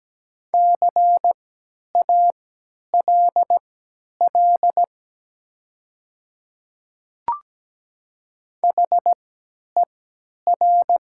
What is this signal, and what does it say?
Morse code, heard as a keyed tone, about 17 wpm, spaced as for 10 wpm, 700 Hz. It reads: CALL HER
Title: 3x Character Spacing - Top 200 Words - 17wpm